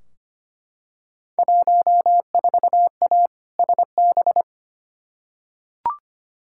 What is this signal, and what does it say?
Morse code, heard as a keyed tone, about 25 wpm, 700 Hz. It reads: 14A SB
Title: Field Day Exchanges Encore - 25wpm